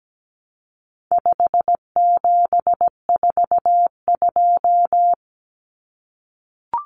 Morse code, heard as a keyed tone, about 17 wpm, 700 Hz. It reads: 5742
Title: Common Cut Numbers - 4-Digit 17wpm